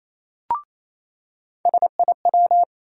Morse code, heard as a keyed tone, about 28 wpm, 700 Hz. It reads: SIW
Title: USA Call Sign Postfixes - 3 Letters 28wpm